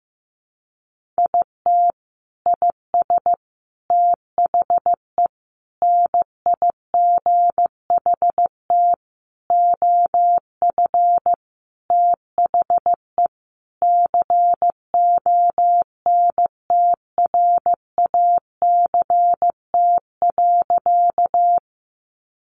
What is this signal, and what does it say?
Morse code, heard as a keyed tone, about 15 wpm, 700 Hz. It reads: IT IS THE NIGHT OF THE CONTRACT.
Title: Sentences from Top 1400 Words 15wpm